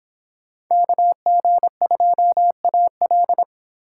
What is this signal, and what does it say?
Morse code, heard as a keyed tone, about 25 wpm, 700 Hz. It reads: KG2AL